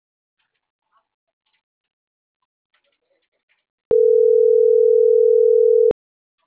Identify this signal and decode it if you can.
Ringback tone (North America), 440 + 480 Hz.